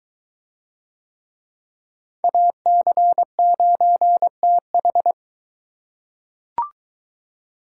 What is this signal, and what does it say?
Morse code, heard as a keyed tone, about 22 wpm, 700 Hz. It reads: AC9TH